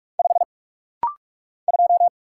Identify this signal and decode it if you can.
Morse code, heard as a keyed tone, about 44 wpm, 700 Hz. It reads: 5 2
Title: Single Letter-Number - Rapid-Fire 45wpm